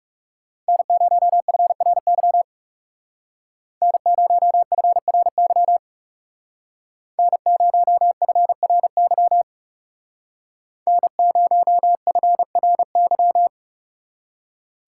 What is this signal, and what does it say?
Morse code, heard as a keyed tone, about 35 wpm, 700 Hz. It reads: N0FRY N0FRY N0FRY N0FRY
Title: Speed Racing - USA 1x3 Call Signs 30wpm